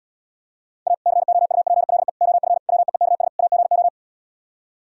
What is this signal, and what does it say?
Morse code, heard as a keyed tone, about 78 wpm, 700 Hz. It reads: A COUPLE OF WEEKS AGO